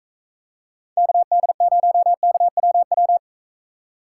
Morse code, heard as a keyed tone, about 45 wpm, 700 Hz. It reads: KD0KWW